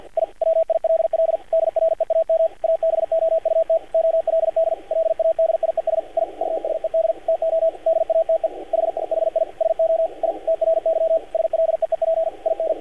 Morse code, heard as a keyed tone, about 46 wpm, 590 Hz, with radio noise. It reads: I WILL DREAM ABOUT OLD LADIES I USED TO DATE H HI SO I TRY SLEEP IN
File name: archived / CRSnet